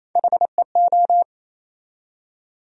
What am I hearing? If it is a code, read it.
Morse code, heard as a keyed tone, about 28 wpm, 700 Hz. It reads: HEO
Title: USA Call Sign Postfixes - 3 Letters 28wpm